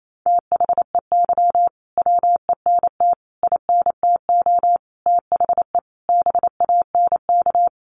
Morse code, heard as a keyed tone, about 28 wpm, 700 Hz. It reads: THEY WENT INTO THE BANK
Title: Sentences from Top 900 Words 28wpm